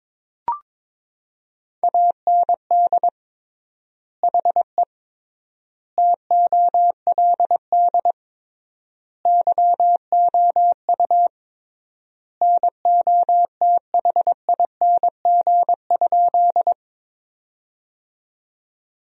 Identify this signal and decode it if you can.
Morse code, heard as a keyed tone, about 22 wpm, 700 Hz. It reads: AND HE TOLD YOU NOTHING?
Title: 3x Word-Spacing - Sentences from Top 500 Words - 22wpm